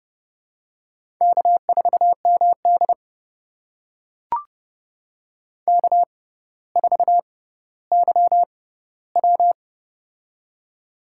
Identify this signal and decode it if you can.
Morse code, heard as a keyed tone, about 30 wpm, spaced as for 15 wpm, 700 Hz. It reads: K4MD K4YW
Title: USA Call Signs - 4 Characters - 6x Character Spacing - 30wpm